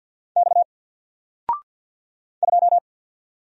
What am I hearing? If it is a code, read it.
Morse code, heard as a keyed tone, about 52 wpm, 700 Hz. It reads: X 2
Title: Single Letter-Number Alt 50wpm